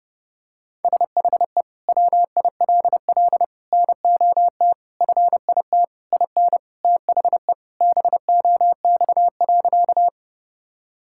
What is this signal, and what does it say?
Morse code, heard as a keyed tone, about 30 wpm, 700 Hz. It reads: SHE WILL NOT FIT IN THE BOX.
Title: Sentences from Top 1000 Words - Review Part 5 - 30wpm